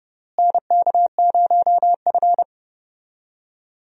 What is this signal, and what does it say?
Morse code, heard as a keyed tone, about 30 wpm, 700 Hz. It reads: NK0F
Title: USA Call Signs - 4 Characters - Repeated 3x - 30wpm